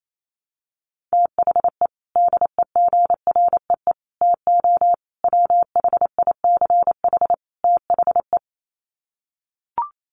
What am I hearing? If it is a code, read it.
Morse code, heard as a keyed tone, about 28 wpm, 700 Hz. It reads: THE DEGREE TO WHICH THE
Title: Sets of 5 Words Encore2 28wpm